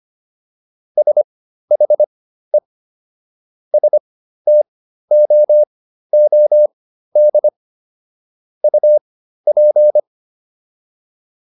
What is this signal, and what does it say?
Morse code, heard as a keyed tone, about 25 wpm, spaced as for 13 wpm, 600 Hz. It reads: SHE STOOD UP